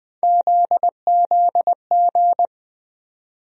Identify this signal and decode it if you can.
Morse code, heard as a keyed tone, about 21 wpm, 700 Hz. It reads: ZZG
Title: USA Call Sign Postfixes - 3 Letters 20wpm